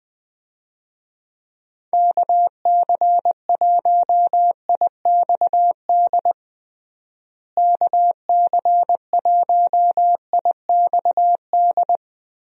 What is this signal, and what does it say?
Morse code, heard as a keyed tone, about 21 wpm, 700 Hz. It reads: KC1IXD KC1IXD
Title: USA Call Signs - 6 Characters - Repeated 3x - 20wpm